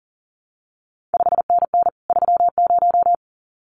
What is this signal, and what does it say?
Morse code, heard as a keyed tone, about 38 wpm, 700 Hz. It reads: 5NN 30
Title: CQ WW DX Contest Exchange Encore - 40wpm